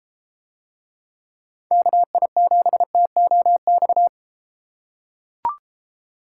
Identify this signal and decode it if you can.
Morse code, heard as a keyed tone, about 33 wpm, 700 Hz. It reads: KI7TOX